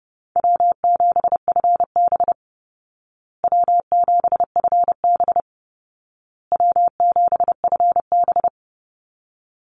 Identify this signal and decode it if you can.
Morse code, heard as a keyed tone, about 30 wpm, 700 Hz. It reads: W7FB W7FB W7FB
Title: Easy Call Signs Repeated 3x - 30wpm